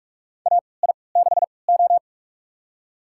Morse code, heard as a keyed tone, about 41 wpm, spaced as for 25 wpm, 690 Hz. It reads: AI6O
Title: USA Call Signs - 4 Characters - 3x Character Spacing - 45wpm